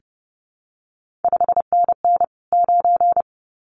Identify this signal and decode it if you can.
Morse code, heard as a keyed tone, about 30 wpm, 700 Hz. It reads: 5NN 9
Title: CQ WW DX Contest Exchange Encore - 30wpm